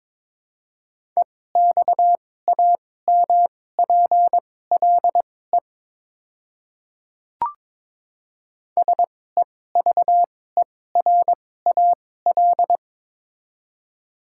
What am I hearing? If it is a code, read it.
Morse code, heard as a keyed tone, about 22 wpm, spaced as for 16 wpm, 700 Hz. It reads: EXAMPLE SEVERAL
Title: 2x Character Spacing - Top 500 Words - 22wpm